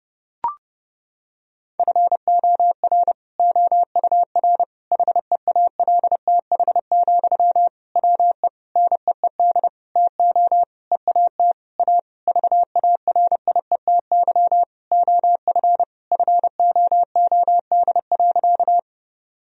Morse code, heard as a keyed tone, about 32 wpm, 700 Hz. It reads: FOR OUR HEALTH, WE NEED TO EAT A VARIETY OF FOOD.
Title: Sentences from Top 1000 Words - Review Part 4 - 30wpm